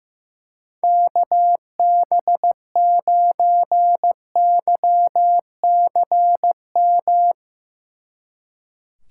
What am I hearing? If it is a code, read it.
Morse code, heard as a keyed tone, about 16 wpm, 700 Hz. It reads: KB9YCM
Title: Speed Racing - USA Call Signs 15wpm